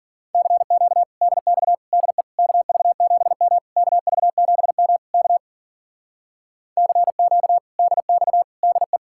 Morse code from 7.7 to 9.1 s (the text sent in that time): DX DE